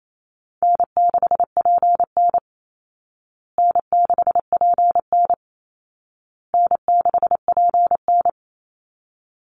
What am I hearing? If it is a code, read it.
Morse code, heard as a keyed tone, about 28 wpm, 700 Hz. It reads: N6PN N6PN N6PN
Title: Easy Call Signs Repeated 3x - 28wpm